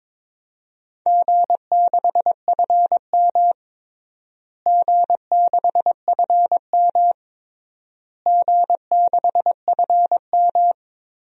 Morse code, heard as a keyed tone, about 22 wpm, 700 Hz. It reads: G6FM G6FM G6FM
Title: DXCC-100 Call Signs - 4 Characters - Repeated 3x - 22wpm